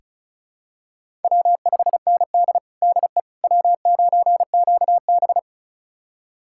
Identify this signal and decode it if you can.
Morse code, heard as a keyed tone, about 36 wpm, 700 Hz. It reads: W5ND DE W9QB
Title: USA Callsign DE Callsign - 4 Characters - 35wpm